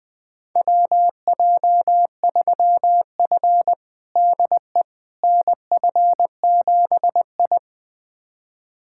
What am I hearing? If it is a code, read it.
Morse code, heard as a keyed tone, about 21 wpm, 700 Hz. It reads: WJ3F DE NF7I
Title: USA Callsign DE Callsign - 4 Characters - 20wpm